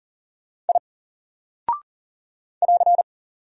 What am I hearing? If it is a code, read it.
Morse code, heard as a keyed tone, about 40 wpm, 700 Hz. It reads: I <AR>
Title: Single Character (Letter:Number:Punctuation:Prosign) 40wpm